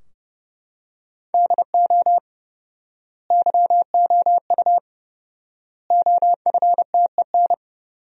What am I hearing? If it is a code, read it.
Morse code, heard as a keyed tone, about 30 wpm, 700 Hz. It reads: DO YOU OFTEN
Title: 4x Word-Spacing - Sentences from Top 300 Words - 30wpm